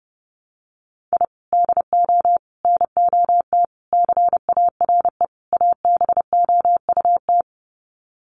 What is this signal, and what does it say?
Morse code, heard as a keyed tone, about 30 wpm, 700 Hz. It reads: I DO NOT CARE ABOUT
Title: Sets of 5 Words Encore2 30wpm